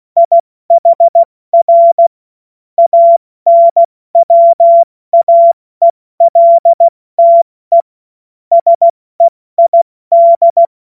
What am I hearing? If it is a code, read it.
Morse code, heard as a keyed tone, about 16 wpm, spaced as for 14 wpm, 680 Hz. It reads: IHR ANWAELTE SEID
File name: Morsekurs / Witze / Witz 02. August 2022 70 F80